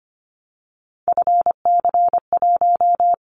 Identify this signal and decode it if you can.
Morse code, heard as a keyed tone, about 25 wpm, 700 Hz. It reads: FC1
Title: Callsign Prefixes for Top 50 DXCC - 25wpm